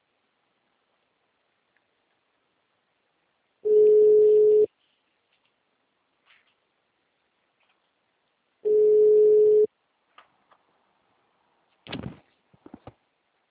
Ringback tone (Europe, 425 Hz).